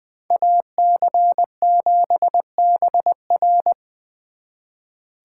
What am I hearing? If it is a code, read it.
Morse code, heard as a keyed tone, about 23 wpm, 700 Hz. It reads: AC7BR
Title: Speed Racing - USA 2x2 Call Signs 20wpm